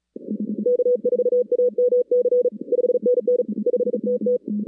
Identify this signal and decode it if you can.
Morse code, heard as a keyed tone, about 37 wpm, 490 Hz. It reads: K4AMC 5NN 5TT
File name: 2016 arrl dx cw / 2016-ARRL-DX-CW-K5TR- - 0-41